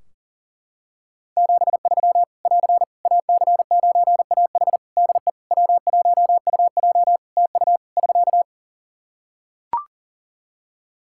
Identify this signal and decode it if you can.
Morse code, heard as a keyed tone, about 40 wpm, 700 Hz. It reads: 73 <AR> AC9AH DE W1UJ TU <SK>
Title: K1USN SST QSO Practice 40wpm